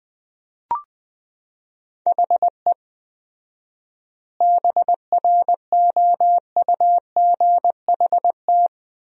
Morse code, heard as a keyed tone, about 20 wpm, 700 Hz. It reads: HE BROUGHT
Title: 4x Word-Spacing - Sentences from Top 500 Words - 20wpm